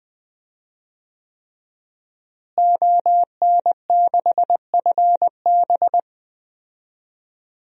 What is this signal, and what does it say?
Morse code, heard as a keyed tone, about 20 wpm, 700 Hz. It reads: ON6FB
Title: DXCC-100 Call Signs - 5 Characters - Repeated 3x - 20wpm